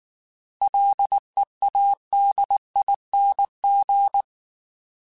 Morse code, heard as a keyed tone, about 19 wpm, 800 Hz. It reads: LEADING
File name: Alphabetical / Alphabetical mp3s / Ls @18